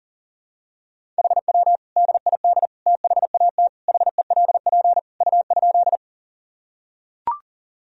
Morse code, heard as a keyed tone, about 40 wpm, 700 Hz. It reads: HW DID THAT HELP U?
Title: Sentences from Top 200 Words and Abbreviations 40wpm